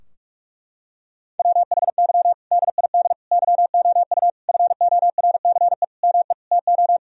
Morse code, heard as a keyed tone, about 41 wpm, 700 Hz. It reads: WHY DID YOU FORCE ME TO
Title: Sentences from Top 1000 Words - Review Part 3 - 45wpm